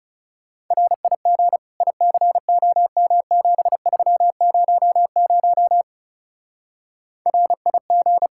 Morse code, heard as a keyed tone, about 39 wpm, 700 Hz. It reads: RIG ICOM7300 RIG